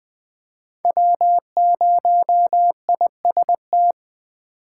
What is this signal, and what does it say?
Morse code, heard as a keyed tone, about 20 wpm, 700 Hz. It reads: W0IST
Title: USA Call Signs - 5 Characters - 8x Character Spacing - 20wpm